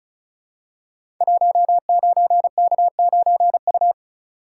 Morse code, heard as a keyed tone, about 36 wpm, 690 Hz. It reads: 19K9U
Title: Code Groups - 5 Letter:Numbers - 35wpm